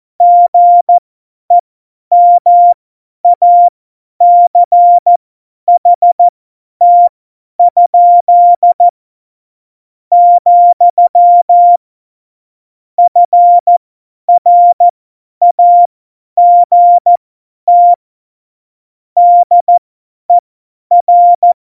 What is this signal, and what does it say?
Morse code, heard as a keyed tone, about 14 wpm, spaced as for 9 wpm, 700 Hz. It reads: GEMACHT?, FRAGT DER
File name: Morsekurs / Witze / Witz 13. Februar 2018, 50 F70